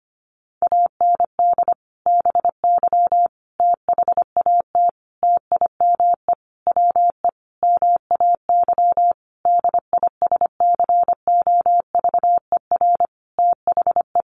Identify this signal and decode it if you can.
Morse code, heard as a keyed tone, about 25 wpm, 700 Hz. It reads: AND BY THAT TIME WE MAY DISCOVER THE
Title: Sentences from Top 1100 Words 25wpm